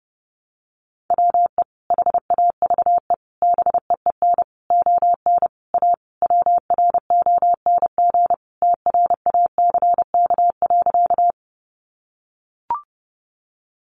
Morse code, heard as a keyed tone, about 30 wpm, 700 Hz. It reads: WE HAVE BEEN ON A WRONG TRACK.
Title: Sentences from Top 1100 Words 30wpm